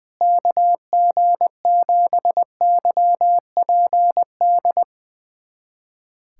Morse code, heard as a keyed tone, about 20 wpm, 700 Hz. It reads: KG7YPD
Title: USA Callsign DE Callsign 20wpm